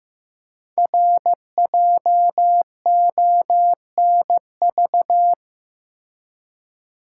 Morse code, heard as a keyed tone, about 15 wpm, 700 Hz. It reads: RJONV
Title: Code Groups - 5 Letters - 15wpm